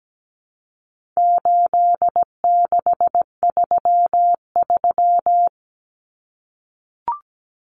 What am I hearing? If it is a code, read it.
Morse code, heard as a keyed tone, about 17 wpm, 700 Hz. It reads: 8633
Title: Numbers - 4-Digit 17wpm